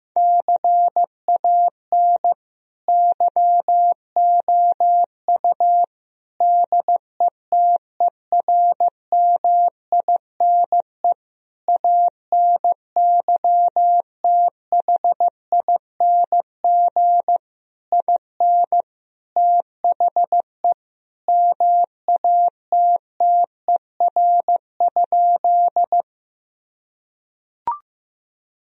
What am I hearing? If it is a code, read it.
Morse code, heard as a keyed tone, about 15 wpm, 700 Hz. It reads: CAN YOU DETERMINE ANYTHING IN THE MATTER?
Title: Sentences from Top 1000 Words - Review Part 4 - 15wpm